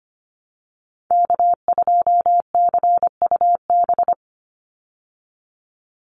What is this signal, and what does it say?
Morse code, heard as a keyed tone, about 25 wpm, 700 Hz. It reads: K2CUB
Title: Call Signs Repeated 3x - 25wpm